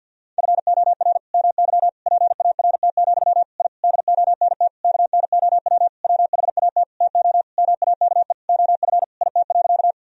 Morse code, heard as a keyed tone, about 52 wpm, 700 Hz. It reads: FOR MY PART, I DONT KNOW WHAT TO MAKE OF IT.
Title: Sentences from Top 1000 Words - Review Part 1 - 50wpm